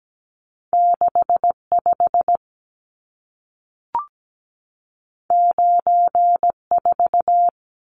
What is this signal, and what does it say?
Morse code, heard as a keyed tone, about 17 wpm, 700 Hz. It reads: 65 94
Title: Numbers - 2-Digit 17wpm